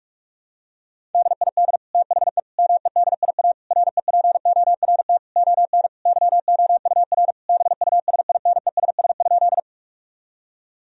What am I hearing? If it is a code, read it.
Morse code, heard as a keyed tone, about 46 wpm, 700 Hz. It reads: DID THE MEDIA REPORT ON YOUR BUSINESS?
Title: Sentences from Top 1000 Words - Review Part 4 - 45wpm